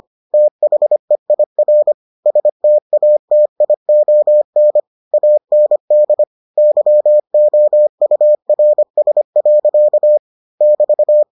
Morse code, heard as a keyed tone, about 25 wpm, 600 Hz. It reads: THEIR STATION AND YOURS. <BT>